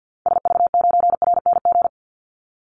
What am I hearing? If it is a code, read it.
Morse code, heard as a keyed tone, about 50 wpm, 700 Hz. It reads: SV9RNG
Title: Call Signs Repeated 3x - 50wpm